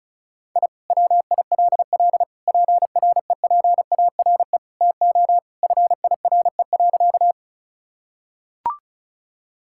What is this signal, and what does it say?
Morse code, heard as a keyed tone, about 36 wpm, 700 Hz. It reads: I WILL PREPARE TO FIRE.
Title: Sentences from Top 1000 Words - Review Part 6 - 35wpm